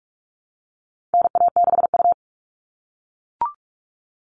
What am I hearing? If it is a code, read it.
Morse code, heard as a keyed tone, about 44 wpm, 700 Hz. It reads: NA6U
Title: All Cut Numbers - 4-Digit 45wpm